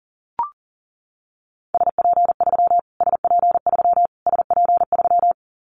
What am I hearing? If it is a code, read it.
Morse code, heard as a keyed tone, about 37 wpm, 700 Hz. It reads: SP3 SP3 SP3
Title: Callsign Prefixes for Top 25 DXCC - Repeated 3x - 40wpm